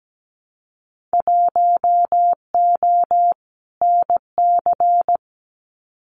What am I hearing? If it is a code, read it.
Morse code, heard as a keyed tone, about 17 wpm, 700 Hz. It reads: 1O NC
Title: Winter Field Day Exchanges 17wpm